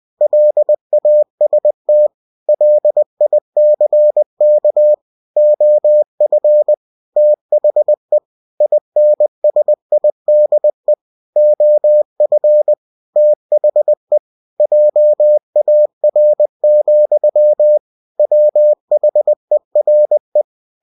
Morse code, heard as a keyed tone, about 20 wpm, 600 Hz. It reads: LAST LICK OF THE INSIDE OF THE JAR, WHERE